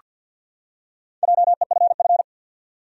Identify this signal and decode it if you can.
Morse code, heard as a keyed tone, about 49 wpm, 700 Hz. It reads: JEFF
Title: USA Common Names - Max 4 Letters - 50wpm